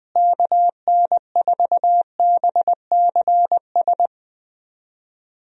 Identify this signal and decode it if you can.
Morse code, heard as a keyed tone, about 21 wpm, 700 Hz. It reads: KN4BCS
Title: USA Call Signs - 6 Characters - Repeated 2x - 20wpm